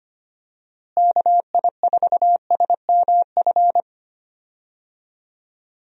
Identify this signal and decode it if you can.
Morse code, heard as a keyed tone, about 25 wpm, 700 Hz. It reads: KI4SMF
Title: USA Call Signs - 6 Characters - Repeated 3x - 25wpm